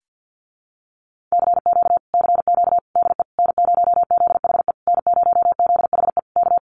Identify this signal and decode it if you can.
Morse code, heard as a keyed tone, about 50 wpm, 700 Hz. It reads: CQ CQ DE N0ZHE N0ZHE K
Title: Common QSO Phrases Encore 50wpm